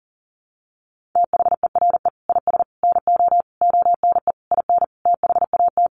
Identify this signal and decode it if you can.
Morse code, heard as a keyed tone, about 40 wpm, 700 Hz. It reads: THERE IS NO ONE IN THAT
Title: Sentences from Top 500 Words 40wpm